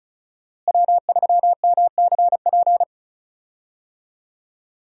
Morse code, heard as a keyed tone, about 34 wpm, 700 Hz. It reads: W3MCP
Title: USA Call Signs - 5 Characters - Repeated 3x - 35wpm